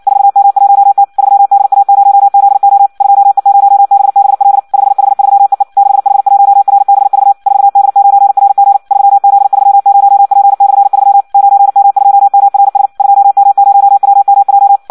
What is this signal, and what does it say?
Morse code, heard as a keyed tone, about 56 wpm, 800 Hz. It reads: 4M1T 3DA0ZO 2E0BBL 5H3EE 6H1RBV 4D9RG 3Z30P/3 9M2MRS 2M0WMJ
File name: Audio practica CW / qrq-tx-58 2